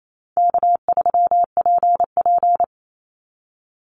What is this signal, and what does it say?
Morse code, heard as a keyed tone, about 28 wpm, 700 Hz. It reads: K3PP
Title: Easy Call Signs - 28wpm